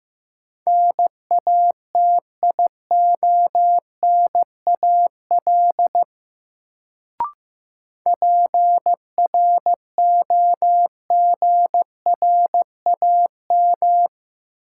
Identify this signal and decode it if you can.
Morse code, heard as a keyed tone, about 15 wpm, 700 Hz. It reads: NATIONAL PROGRAM
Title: Sentences from Top 1000 Words - Review Part 4 - New Word Drill - 15wpm